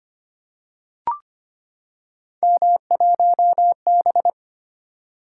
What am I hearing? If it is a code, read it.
Morse code, heard as a keyed tone, about 25 wpm, 700 Hz. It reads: M1B
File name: Three Letter-Number Combinations - 25wpm